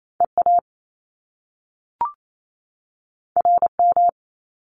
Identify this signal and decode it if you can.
Morse code, heard as a keyed tone, about 28 wpm, 700 Hz. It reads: EA RM
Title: Top Two Letter Combinations in Words 28wpm